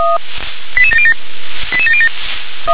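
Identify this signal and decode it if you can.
Five-tone selective call (ZVEI), heard as quick runs of five tones, tones of about 70 ms. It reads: 88707 88707
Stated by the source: ZVEI five-tone selective call 88705, received by SDR.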